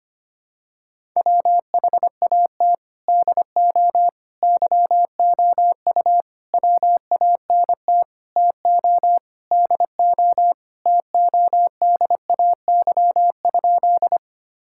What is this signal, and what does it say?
Morse code, heard as a keyed tone, about 25 wpm, 700 Hz. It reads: WHAT DO YOU WANT TO DO TODAY?
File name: practice / 25wpm / Sentences with 4-Word Phrases - 25wpm